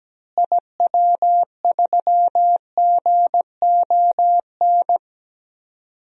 Morse code, heard as a keyed tone, about 17 wpm, 700 Hz. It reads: IW3GON
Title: DXCC-100 Call Signs - 6 Characters - Repeated 3x - 17wpm